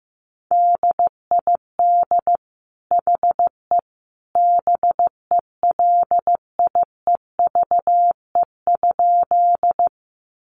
Morse code, heard as a keyed tone, about 15 wpm, 700 Hz. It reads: DID HE BELIEVE?
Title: Sentences from Top 1000 Words 15wpm